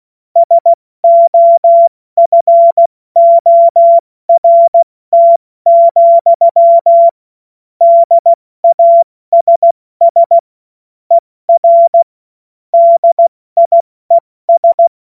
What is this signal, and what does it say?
Morse code, heard as a keyed tone, about 16 wpm, 680 Hz. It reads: SOFORT, DASS ER DIES
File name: Morsekurs / Witze / Witz 30. August 2022 70 F80